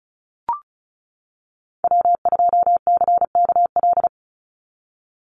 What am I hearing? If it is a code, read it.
Morse code, heard as a keyed tone, about 34 wpm, 700 Hz. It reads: W2CKL
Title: NA SOTA Activator Callsigns 35wpm